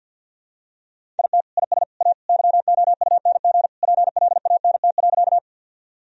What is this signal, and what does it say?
Morse code, heard as a keyed tone, about 49 wpm, 700 Hz. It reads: IT IS A YOUNG PLANT.
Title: Sentences from Top 1000 Words - Review Part 1 - 50wpm